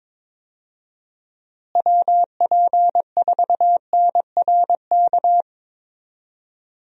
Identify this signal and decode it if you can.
Morse code, heard as a keyed tone, about 22 wpm, 700 Hz. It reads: WP4NRK
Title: USA Call Signs - 6 Characters - Repeated 3x - 22wpm